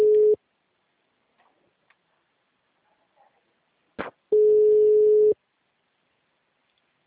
Ringback tone (Europe, 425 Hz).